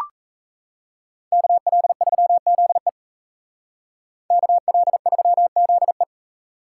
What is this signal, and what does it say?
Morse code, heard as a keyed tone, about 36 wpm, 700 Hz. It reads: KL3ZE KL3ZE